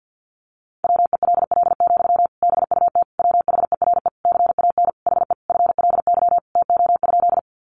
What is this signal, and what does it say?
Morse code, heard as a keyed tone, about 50 wpm, 700 Hz. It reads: WELL, BUT WHERE CAN HE FLY TO?
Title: Sentences from Top 500 Words 50wpm